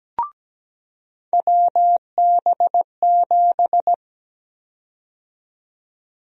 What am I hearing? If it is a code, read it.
Morse code, heard as a keyed tone, about 17 wpm, 700 Hz. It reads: WB7
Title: USA Call Sign Prefixes - 17wpm